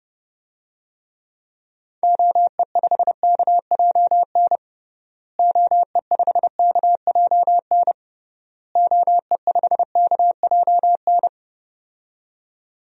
Morse code, heard as a keyed tone, about 32 wpm, 700 Hz. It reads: OE5KJN OE5KJN OE5KJN
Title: DXCC-100 Call Signs - 6 Characters - Repeated 3x - 30wpm